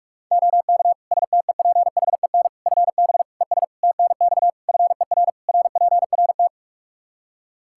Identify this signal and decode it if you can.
Morse code, heard as a keyed tone, about 46 wpm, 700 Hz. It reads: OK STEPHEN FB ES TNX FER RPRT
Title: Practice QSO - Exchange 2 - Info - 45wpm